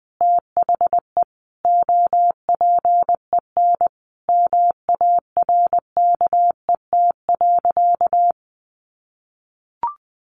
Morse code, heard as a keyed tone, about 20 wpm, 700 Hz. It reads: THE OPEN MARKET.
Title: Sentences from Top 1400 Words 20wpm